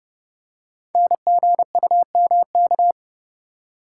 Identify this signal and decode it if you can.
Morse code, heard as a keyed tone, about 30 wpm, 700 Hz. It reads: NGUMK
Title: Code Groups - 5 Letters - 30wpm